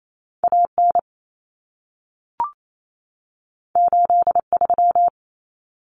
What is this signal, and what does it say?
Morse code, heard as a keyed tone, about 28 wpm, 700 Hz. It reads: AN 83
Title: Common Cut Numbers - 2-Digit 28wpm